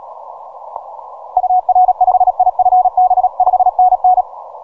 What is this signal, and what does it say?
Morse code, heard as a keyed tone, about 37 wpm, 740 Hz. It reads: UR5IFB 5NN
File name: simple-file-list / VP8STI 12MCW